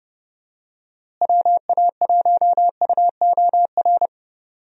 Morse code, heard as a keyed tone, about 30 wpm, 700 Hz. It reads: WA1UOR
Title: USA Call Signs - 6 Characters - Repeated 3x - 30wpm